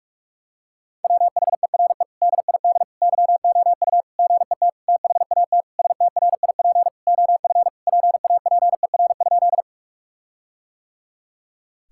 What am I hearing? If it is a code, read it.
Morse code, heard as a keyed tone, about 41 wpm, 700 Hz. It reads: WHERE DID YOU GET THAT STRIP OF PAPER?
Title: Sentences from Top 1600 Words 45wpm